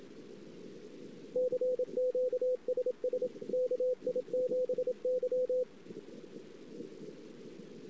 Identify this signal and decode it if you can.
Morse code, heard as a keyed tone, about 27 wpm, 500 Hz. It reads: CQ SS KI7Y
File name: ss-cw